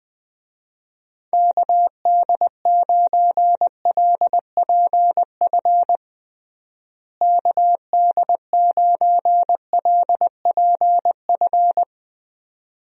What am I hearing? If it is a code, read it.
Morse code, heard as a keyed tone, about 21 wpm, 700 Hz. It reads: KD9LPF KD9LPF
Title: USA Call Signs - 6 Characters - Repeated 2x - 20wpm